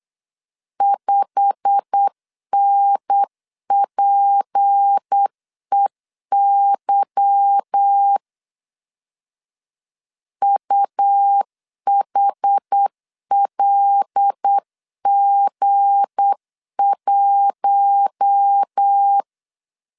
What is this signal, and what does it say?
Morse code, heard as a keyed tone, about 8 wpm, 800 Hz. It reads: 5NPEY UHLG1
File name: LEZIONI / 9 286 8wpm 3425